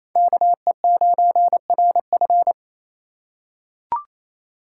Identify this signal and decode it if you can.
Morse code, heard as a keyed tone, about 28 wpm, 700 Hz. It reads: KE9RF
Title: USA Call Signs - 6x Character Spacing - 28wpm